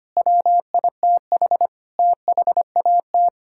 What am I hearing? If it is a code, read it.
Morse code, heard as a keyed tone, about 25 wpm, 700 Hz. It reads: WITH THAT
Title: Sentences from Top 1000 Words - Review Part 5 - 25wpm